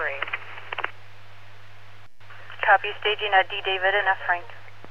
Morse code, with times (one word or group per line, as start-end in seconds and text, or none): none